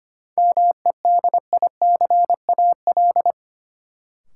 Morse code, heard as a keyed tone, about 25 wpm, 700 Hz. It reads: MEDICAL
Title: Sentences from Top 800 Words - New Word Drill - 25wpm